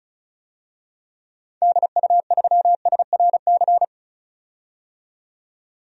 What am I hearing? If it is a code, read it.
Morse code, heard as a keyed tone, about 36 wpm, 700 Hz. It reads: DU3SRC
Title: DXCC-100 Call Signs - 6 Characters - 35wpm